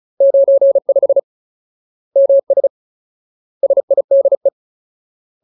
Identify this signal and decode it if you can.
Morse code, heard as a keyed tone, about 35 wpm, 550 Hz. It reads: 95 MS SIDE
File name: Ultimate Word List / 35WPM Ultimate